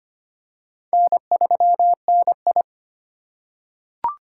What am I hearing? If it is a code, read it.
Morse code, heard as a keyed tone, about 25 wpm, 690 Hz. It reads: N3NI
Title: USA Call Signs - 4 Characters - 4x Character Spacing - 25wpm